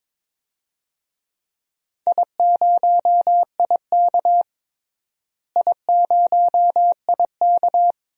Morse code, heard as a keyed tone, about 22 wpm, 700 Hz. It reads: I0IK I0IK
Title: DXCC-100 Call Signs - 4 Characters - Repeated 2x - 22wpm